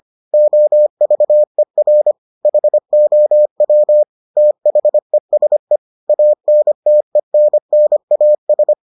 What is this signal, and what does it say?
Morse code, heard as a keyed tone, about 25 wpm, 600 Hz. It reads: OVER HOW THESE ANTENNAS